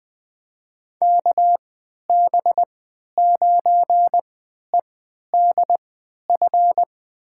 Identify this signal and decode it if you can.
Morse code, heard as a keyed tone, about 20 wpm, spaced as for 11 wpm, 700 Hz. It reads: KB9EDF
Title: USA Call Signs - 6 Characters - 3x Character Spacing - 20wpm